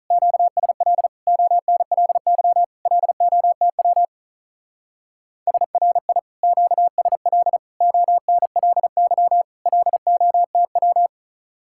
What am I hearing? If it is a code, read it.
Morse code, heard as a keyed tone, about 44 wpm, 700 Hz. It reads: QSL ONLY LOTW SRI QSL ONLY LOTW